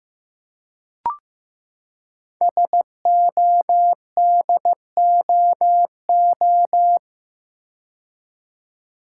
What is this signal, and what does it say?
Morse code, heard as a keyed tone, about 15 wpm, 700 Hz. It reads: SODOO
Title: Code Groups - 5 Letters - 15wpm